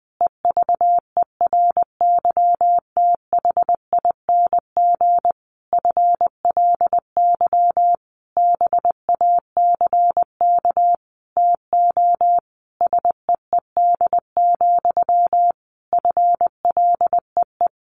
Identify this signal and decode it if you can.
Morse code, heard as a keyed tone, about 20 wpm, 700 Hz. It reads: EVERYTHING FLY BACK TO SEED, FLEE
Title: A Sound Of Thunder 20wpm